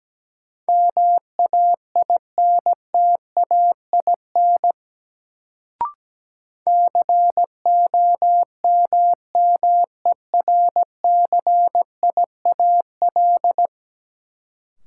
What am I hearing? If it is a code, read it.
Morse code, heard as a keyed tone, about 17 wpm, 700 Hz. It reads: MAINTAIN COMMERCIAL
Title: Sentences from Top 1000 Words - Review Part 5 - New Word Drill - 17wpm